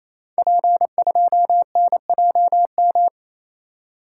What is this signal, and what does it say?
Morse code, heard as a keyed tone, about 28 wpm, 700 Hz. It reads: P2NJM
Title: Code Groups - 5 Letter:Numbers - 28wpm